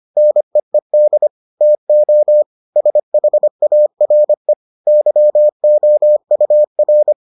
Morse code, heard as a keyed tone, about 25 wpm, 600 Hz. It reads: NEED TO SHARE YOUR